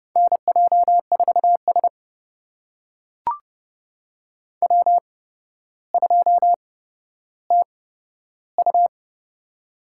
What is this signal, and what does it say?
Morse code, heard as a keyed tone, about 32 wpm, spaced as for 13 wpm, 700 Hz. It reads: NJ4S W2TU
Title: USA Call Signs - 8x Character Spacing - 30wpm